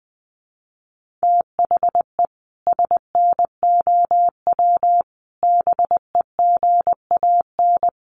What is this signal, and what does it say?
Morse code, heard as a keyed tone, about 20 wpm, 700 Hz. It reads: THE SNOW BEGAN
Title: Sentences from Top 1500 Words 20wpm